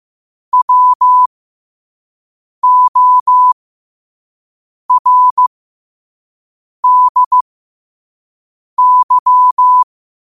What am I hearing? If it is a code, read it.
Morse code, heard as a keyed tone, about 15 wpm, spaced as for 5 wpm, 1000 Hz. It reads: WORDY
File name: qotdfiles / 5wpm